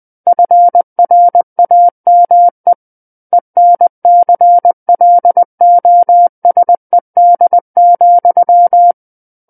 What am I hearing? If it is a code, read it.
Morse code, heard as a keyed tone, about 20 wpm, 700 Hz. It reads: FRAME ENCLOSED,
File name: 20 WPM